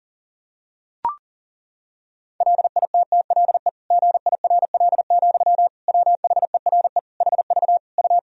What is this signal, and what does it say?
Morse code, heard as a keyed tone, about 40 wpm, 700 Hz. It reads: LITTLE GIRL, WHERE HV U
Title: Sentences from Top 400 Words and Abbreviations 40wpm